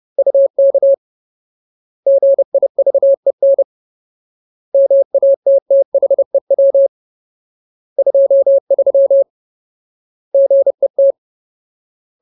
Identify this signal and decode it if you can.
Morse code, heard as a keyed tone, about 30 wpm, 550 Hz. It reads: UK GIVEN MATTHEW 23 GET
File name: Ultimate Word List / 30WPM Ultimate